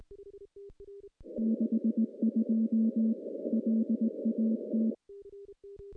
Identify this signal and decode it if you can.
Morse code, heard as a keyed tone, about 19 wpm, 220 Hz. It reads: 62 LAT
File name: contests / CW-SS-k5tr- -0200